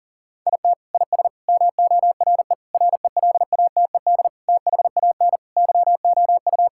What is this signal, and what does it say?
Morse code, heard as a keyed tone, about 40 wpm, 700 Hz. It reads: IT IS MORE RELATED THAN YOU
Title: Sentences from Top 1000 Words - Review Part 5 - 40wpm